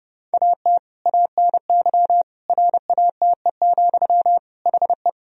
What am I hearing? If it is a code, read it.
Morse code, heard as a keyed tone, about 30 wpm, 700 Hz. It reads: AT ANY RATE, HE
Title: Sentences from Top 1000 Words - Review Part 4 - 30wpm